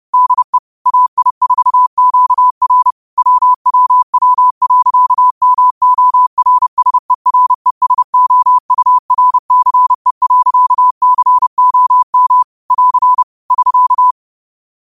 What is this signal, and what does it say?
Morse code, heard as a keyed tone, about 30 wpm, 1000 Hz. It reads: DE AI4QR WWW.MORSERESOURCE.COM <AR> <SK>